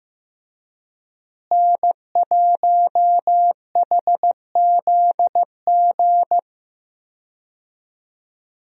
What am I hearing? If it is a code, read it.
Morse code, heard as a keyed tone, about 15 wpm, 700 Hz. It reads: N1HZG
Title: USA Call Signs - 5 Characters - Repeated 3x - 15wpm